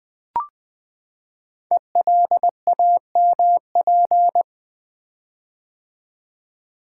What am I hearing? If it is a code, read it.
Morse code, heard as a keyed tone, about 20 wpm, 700 Hz. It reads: ELAMP
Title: Code Groups - 5 Letters - 20wpm